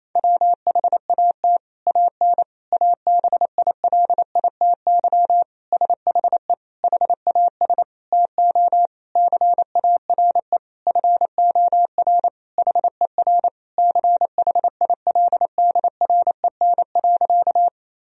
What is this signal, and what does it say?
Morse code, heard as a keyed tone, about 28 wpm, 700 Hz. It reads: WHAT AN ABILITY SHE HAS TO CARE FOR HER CHILDREN.
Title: Sentences from Top 1000 Words - Review Part 3 - 28wpm